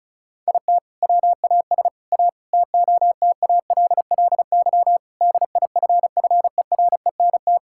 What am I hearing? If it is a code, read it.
Morse code, heard as a keyed tone, about 36 wpm, 700 Hz. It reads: IT WAS A TOTALLY DIFFERENT
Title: Sentences from Top 1600 Words 35wpm